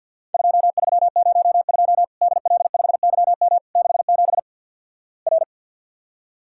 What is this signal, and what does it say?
Morse code, heard as a keyed tone, about 53 wpm, 700 Hz. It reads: 2302 DL5YM 67 R
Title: Worked All Europe DX Contest - QTC Exchange - 50wpm